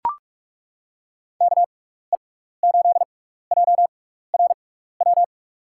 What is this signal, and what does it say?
Morse code, heard as a keyed tone, about 42 wpm, spaced as for 15 wpm, 700 Hz. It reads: KE8JRW
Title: USA Call Signs - 6 Characters - 6x Character Spacing - 45wpm